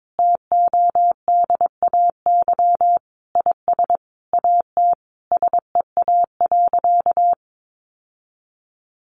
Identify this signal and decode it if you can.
Morse code, heard as a keyed tone, about 22 wpm, 700 Hz. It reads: TODAY IS AT SEA.
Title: Sentences from Top 1400 Words 22wpm